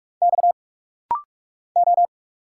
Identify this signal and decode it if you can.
Morse code, heard as a keyed tone, about 45 wpm, 690 Hz. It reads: X O
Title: Single Letters - Rapid-Fire 45wpm